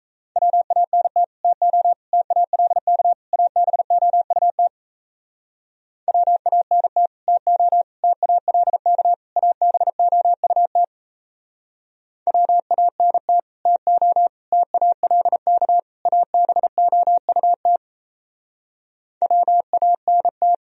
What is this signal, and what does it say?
Morse code, heard as a keyed tone, about 34 wpm, 700 Hz. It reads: WANT TO TALK ABOUT WANT TO TALK ABOUT WANT TO TALK ABOUT WANT
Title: Speed-Racing - Sets of 4 Words 28wpm